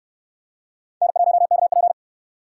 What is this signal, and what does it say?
Morse code, heard as a keyed tone, about 69 wpm, 700 Hz. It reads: N1KP